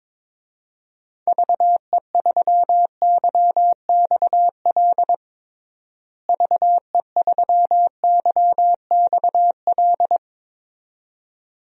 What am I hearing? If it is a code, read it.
Morse code, heard as a keyed tone, about 22 wpm, 700 Hz. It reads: VE3YXL VE3YXL
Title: DXCC-100 Call Signs - 6 Characters - Repeated 2x - 22wpm